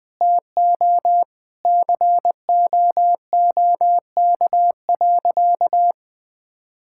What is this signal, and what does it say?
Morse code, heard as a keyed tone, about 21 wpm, 700 Hz. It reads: TO COOK.
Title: Sentences with 3-Word Phrases - 20wpm